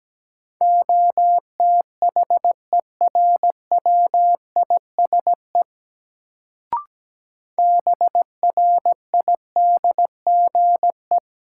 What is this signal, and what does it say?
Morse code, heard as a keyed tone, about 17 wpm, 700 Hz. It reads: OTHERWISE BRIDGE
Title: Sentences from Top 1200 Words - New Word Drill - 17wpm